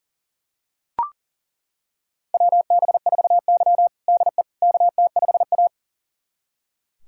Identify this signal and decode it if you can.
Morse code, heard as a keyed tone, about 40 wpm, 700 Hz. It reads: WB4Y DE KT5A
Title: USA Callsign DE Callsign - 4 Characters - 40wpm